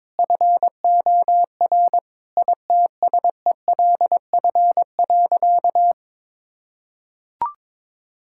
Morse code, heard as a keyed tone, about 22 wpm, 700 Hz. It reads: FOR ITSELF.
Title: Sentences from Top 1000 Words - Review Part 5 - 22wpm